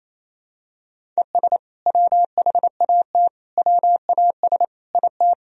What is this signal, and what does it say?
Morse code, heard as a keyed tone, about 28 wpm, 700 Hz. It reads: ES WHAT WAS IT
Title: Sentences from Top 100 Words and Abbreviations 28wpm